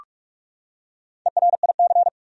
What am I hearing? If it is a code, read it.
Morse code, heard as a keyed tone, about 46 wpm, 700 Hz. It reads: ERIC
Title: USA Common Names - Max 4 Letters - 45wpm